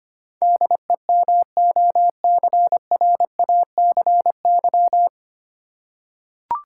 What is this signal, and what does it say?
Morse code, heard as a keyed tone, about 25 wpm, 700 Hz. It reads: DEMOCRACY
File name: Nine Letter Words - 25wpm